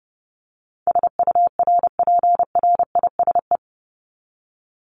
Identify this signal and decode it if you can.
Morse code, heard as a keyed tone, about 30 wpm, 700 Hz. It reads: SURPRISE
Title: Top 1751-2000 Words 30wpm